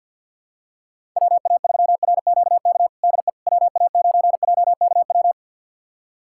Morse code, heard as a keyed tone, about 53 wpm, 700 Hz. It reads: WA3RQK DE WA9JKW
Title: USA Callsign DE Callsign - 6 Characters - 50wpm